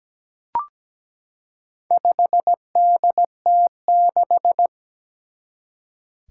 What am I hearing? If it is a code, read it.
Morse code, heard as a keyed tone, about 17 wpm, 700 Hz. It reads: 5DT6
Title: Four Challenging Letter-Number Combinations 17wpm